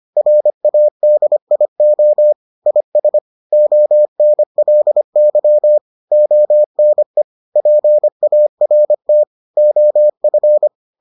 Morse code, heard as a keyed tone, about 25 wpm, 600 Hz. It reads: RADIO IS ONLY ONE PART OF